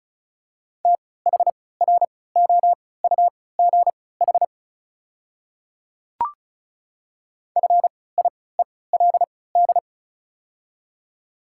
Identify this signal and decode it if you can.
Morse code, heard as a keyed tone, about 36 wpm, spaced as for 20 wpm, 700 Hz. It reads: THROUGH FIELD
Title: 3x Character Spacing - Top 500 Words - 35wpm